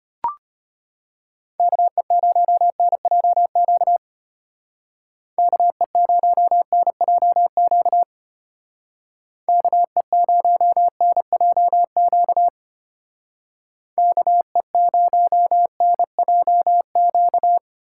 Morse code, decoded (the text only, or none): KE0NJQ KE0NJQ KE0NJQ KE0NJQ